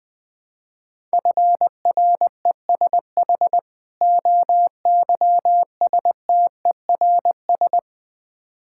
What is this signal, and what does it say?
Morse code, heard as a keyed tone, about 20 wpm, 700 Hz. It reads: FRESH OYSTERS
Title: Difficult Words with Hints - 20wpm